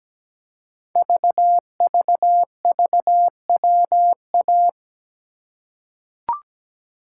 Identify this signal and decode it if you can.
Morse code, heard as a keyed tone, about 17 wpm, 700 Hz. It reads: VVVWA
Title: Code Groups - 5 Letters - 17wpm